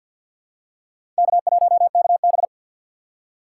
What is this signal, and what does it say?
Morse code, heard as a keyed tone, about 49 wpm, 700 Hz. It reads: K1KB
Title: USA Call Signs - 4 Characters - 3x Character Spacing - 50wpm